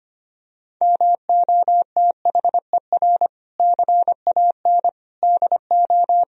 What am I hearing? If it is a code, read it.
Morse code, heard as a keyed tone, about 25 wpm, 700 Hz. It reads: MOTHER CAN DO
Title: Sentences from Top 1000 Words - Review Part 4 - 25wpm